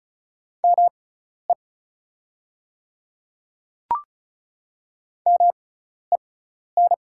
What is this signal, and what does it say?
Morse code, heard as a keyed tone, about 36 wpm, spaced as for 12 wpm, 690 Hz. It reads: ME MEN
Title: 6x Character Spacing - Top 500 Words - 35wpm